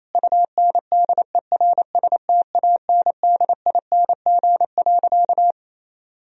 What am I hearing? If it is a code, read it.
Morse code, heard as a keyed tone, about 28 wpm, 700 Hz. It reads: UNDERSTANDING.
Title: Sentences from Top 1000 Words - Review Part 3 - 28wpm